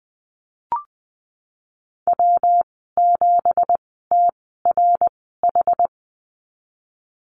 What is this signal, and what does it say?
Morse code, heard as a keyed tone, about 20 wpm, spaced as for 15 wpm, 700 Hz. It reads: W7TRH
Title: 179 Lesson 29g - Slash - Call Signs 20wpm